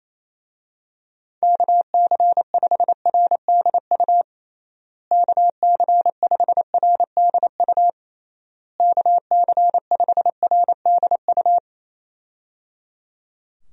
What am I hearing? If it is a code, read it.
Morse code, heard as a keyed tone, about 28 wpm, 700 Hz. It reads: KC5RDU KC5RDU KC5RDU
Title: USA Call Signs - 6 Characters - Repeated 3x - 28wpm